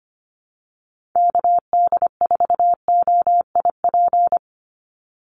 Morse code, heard as a keyed tone, about 25 wpm, 700 Hz. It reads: KD4OIP
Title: USA Call Signs 25wpm